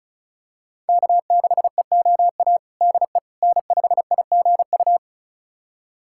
Morse code, heard as a keyed tone, about 36 wpm, 700 Hz. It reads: K6EOA DE N5IGU
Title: USA Callsign DE Callsign - 5 Characters - 35wpm